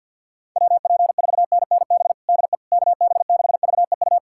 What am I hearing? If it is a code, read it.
Morse code, heard as a keyed tone, about 51 wpm, 700 Hz. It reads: WP4NNB DE KB6VEU
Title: USA Callsign DE Callsign - 6 Characters - 50wpm